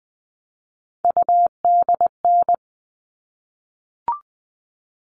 Morse code, heard as a keyed tone, about 20 wpm, 700 Hz. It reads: UDN
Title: All Cut Numbers - 3-Digit 20wpm